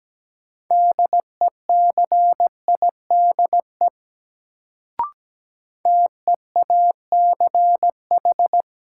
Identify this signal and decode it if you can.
Morse code, heard as a keyed tone, about 17 wpm, 700 Hz. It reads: DECIDE TEACH
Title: Sentences from Top 500 Words - New Word Drill - 17wpm